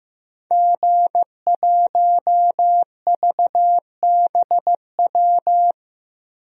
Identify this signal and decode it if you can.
Morse code, heard as a keyed tone, about 15 wpm, 700 Hz. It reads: G1VBW
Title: DXCC-100 Call Signs - 5 Characters - Repeated 2x - 15wpm